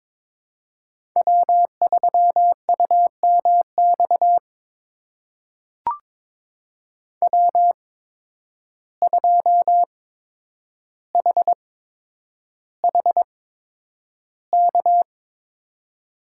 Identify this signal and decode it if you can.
Morse code, heard as a keyed tone, about 22 wpm, spaced as for 9 wpm, 700 Hz. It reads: W3UMX W2HHK
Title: USA Call Signs - 5 Characters - 8x Character Spacing - 22wpm